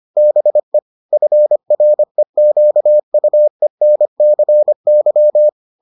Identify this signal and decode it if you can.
Morse code, heard as a keyed tone, about 25 wpm, 600 Hz. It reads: BE FREQUENCY